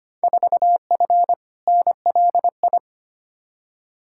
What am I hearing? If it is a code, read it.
Morse code, heard as a keyed tone, about 25 wpm, 700 Hz. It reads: 4F NLI
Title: Field Day Exchanges Encore - 25wpm